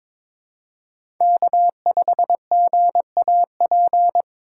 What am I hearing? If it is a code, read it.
Morse code, heard as a keyed tone, about 22 wpm, 700 Hz. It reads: K5GAP
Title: USA Call Signs - 5 Characters - 6x Character Spacing - 22wpm